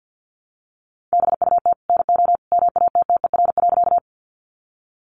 Morse code, heard as a keyed tone, about 50 wpm, 700 Hz. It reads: BUT NO MATTER.
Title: Sentences from Top 700 Words 50wpm